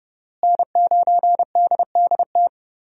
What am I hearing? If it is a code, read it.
Morse code, heard as a keyed tone, about 31 wpm, 700 Hz. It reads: N9DDT
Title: USA Call Signs - 5 Characters - Repeated 3x - 30wpm